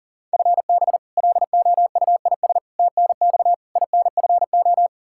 Morse code, heard as a keyed tone, about 40 wpm, 700 Hz. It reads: FB LOUIS TNX INFO
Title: K1USN SST QSO Practice 40wpm